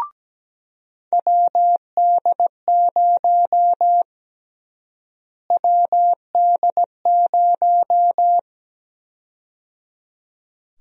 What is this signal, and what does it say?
Morse code, heard as a keyed tone, about 17 wpm, 700 Hz. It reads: WD0 WD0
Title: USA Call Sign Prefixes - Repeated 2x - 17wpm